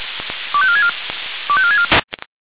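Five-tone selective call (ZVEI), heard as quick runs of five tones, tones of about 70 ms. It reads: 25464 25464
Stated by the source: ZVEI five-tone selective call received with an SDR.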